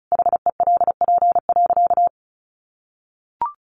Morse code, heard as a keyed tone, about 34 wpm, 700 Hz. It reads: HELP.
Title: Sentences from Top 1100 Words 35wpm